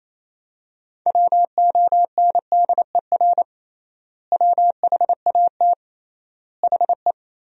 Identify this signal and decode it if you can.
Morse code, heard as a keyed tone, about 28 wpm, 700 Hz. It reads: WONDER WHAT HE
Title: 3x Word-Spacing - Sentences from Top 500 Words - 28wpm